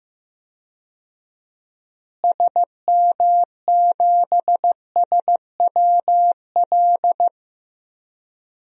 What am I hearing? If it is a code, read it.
Morse code, heard as a keyed tone, about 16 wpm, 700 Hz. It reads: SM7SWL
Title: DXCC-100 Call Signs - 6 Characters - Repeated 2x - 15wpm